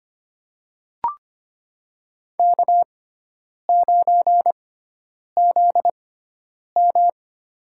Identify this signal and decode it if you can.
Morse code, heard as a keyed tone, about 25 wpm, spaced as for 9 wpm, 700 Hz. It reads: K9ZM
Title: USA Call Signs - 4 Characters - 6x Character Spacing - 25wpm